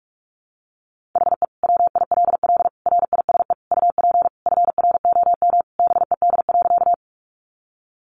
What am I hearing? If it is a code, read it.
Morse code, heard as a keyed tone, about 44 wpm, 700 Hz. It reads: HE WILL RISE UP FROM BED.
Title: Sentences from Top 900 Words 45wpm